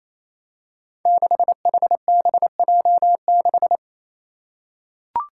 Morse code, heard as a keyed tone, about 28 wpm, 700 Hz. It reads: 6HBJ6
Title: Five Challenging Letter-Number Combinations 28wpm